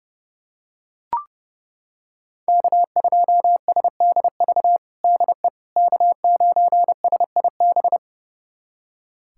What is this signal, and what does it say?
Morse code, heard as a keyed tone, about 30 wpm, 700 Hz. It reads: K2SDV DE K9SIB
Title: USA Callsign DE Callsign - 5 Characters - 30wpm